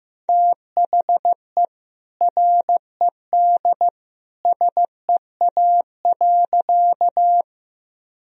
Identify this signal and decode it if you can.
Morse code, heard as a keyed tone, about 16 wpm, 700 Hz. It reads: THE RED SEA.
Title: Sentences from Top 1000 Words - Review Part 2 - 15wpm